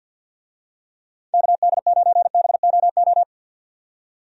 Morse code, heard as a keyed tone, about 52 wpm, 700 Hz. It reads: KD9BOO
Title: USA Call Signs - 6 Characters - 50wpm